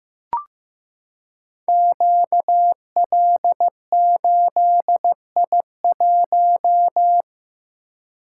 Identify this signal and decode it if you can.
Morse code, heard as a keyed tone, about 15 wpm, 700 Hz. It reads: QL8I1
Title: Code Groups - 5 Letter:Numbers - 15wpm